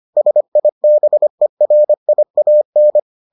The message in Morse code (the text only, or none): SIBERIAN